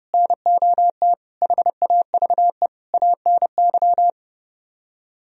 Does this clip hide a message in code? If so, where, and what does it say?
Yes: Morse code from 0.1 to 4.1 s, heard as a keyed tone, about 31 wpm, 700 Hz. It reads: NOT HAVE ANY